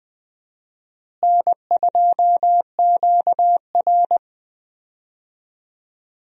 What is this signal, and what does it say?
Morse code, heard as a keyed tone, about 20 wpm, 700 Hz. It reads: N2QR
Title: USA Call Signs - 4 Characters - Repeated 3x - 20wpm